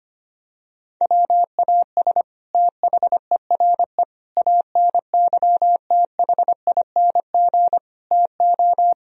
Morse code, heard as a keyed tone, about 25 wpm, 700 Hz. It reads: WAS THERE ANYTHING TO
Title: Sentences from Top 1000 Words - Review Part 1 - 25wpm